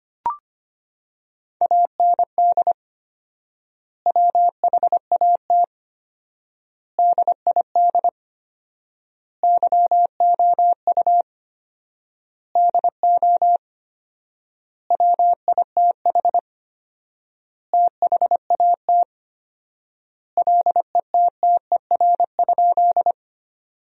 Morse code, heard as a keyed tone, about 25 wpm, 700 Hz. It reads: AND WHAT DID YOU DO WITH THAT LETTER?
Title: 4x Word-Spacing - Sentences from Top 300 Words - 25wpm